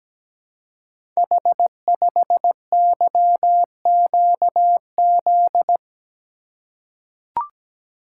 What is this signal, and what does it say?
Morse code, heard as a keyed tone, about 17 wpm, 700 Hz. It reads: H5YQZ
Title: Code Groups - 5 Letter:Numbers - 17wpm